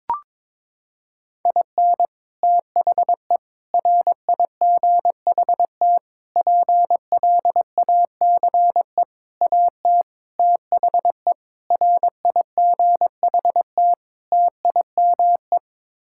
Morse code, heard as a keyed tone, about 22 wpm, 700 Hz. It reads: IN THE RIGHT PLACE AT THE RIGHT TIME
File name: practice / 22wpm / Sets of 8 Words - 22wpm